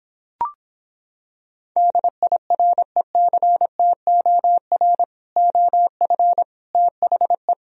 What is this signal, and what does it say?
Morse code, heard as a keyed tone, about 26 wpm, 700 Hz. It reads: DIRECTOR OF THE